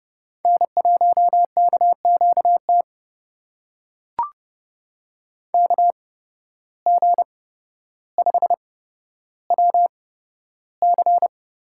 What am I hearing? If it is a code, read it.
Morse code, heard as a keyed tone, about 30 wpm, spaced as for 13 wpm, 700 Hz. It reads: N1KQT KG5WC
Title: USA Call Signs - 8x Character Spacing - 30wpm